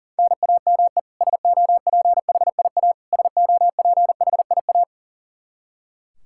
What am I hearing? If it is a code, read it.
Morse code, heard as a keyed tone, about 37 wpm, 700 Hz. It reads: NAME SOPHIA SOPHIA
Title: Practice QSO - Exchange 1 - Beginning - 40wpm